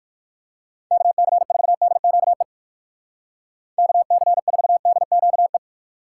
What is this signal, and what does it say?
Morse code, heard as a keyed tone, about 49 wpm, 700 Hz. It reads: KC4DQE KC4DQE
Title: Speed Racing - USA 2x3 Call Signs 45wpm